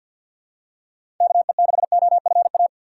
Morse code, heard as a keyed tone, about 52 wpm, 700 Hz. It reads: KE6OFA
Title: USA Call Signs - 3x Character Spacing - 50wpm